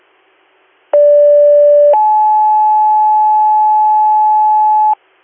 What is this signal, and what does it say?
Two-tone page, heard as a one-second tone, then a three-second tone, about 580 then 860 Hz, with noise. It is heard from 0.9 to 4.9 s.